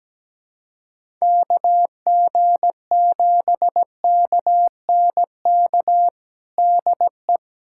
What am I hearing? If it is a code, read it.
Morse code, heard as a keyed tone, about 17 wpm, 700 Hz. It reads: KG7KNK DE
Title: USA Callsign DE Callsign - 6 Characters - 17wpm